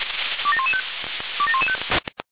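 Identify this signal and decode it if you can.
Five-tone selective call (ZVEI), heard as quick runs of five tones, tones of about 70 ms. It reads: 27115 27115
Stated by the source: ZVEI five-tone selective call received by SDR.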